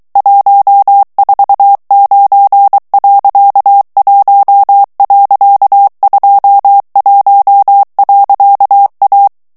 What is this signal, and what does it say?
Morse code, heard as a keyed tone, about 23 wpm, 800 Hz. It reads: 149.1.21.A
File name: morse mp3 / ESPAGNOL TEXTE EN CLAIR 2 VITESSE 22 MOTS MINUTES